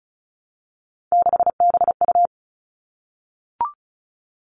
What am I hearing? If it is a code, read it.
Morse code, heard as a keyed tone, about 36 wpm, 700 Hz. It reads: 6BU
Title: Three Challenging Letter-Number Combinations 35wpm